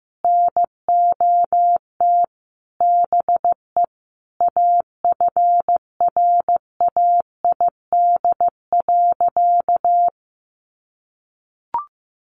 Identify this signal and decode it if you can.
Morse code, heard as a keyed tone, about 15 wpm, 700 Hz. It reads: NOT BE AFRAID.
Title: Sentences from Top 1500 Words 15wpm